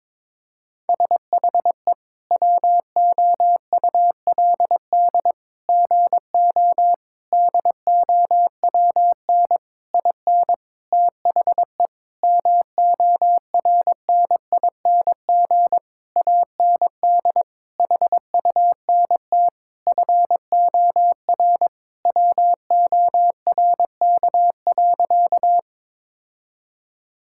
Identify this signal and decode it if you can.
Morse code, heard as a keyed tone, about 22 wpm, 700 Hz. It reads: SHE WOULD GO DOWN IN THE MORNING AND HUNT FOR WORK.
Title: Sentences from Top 1600 Words 22wpm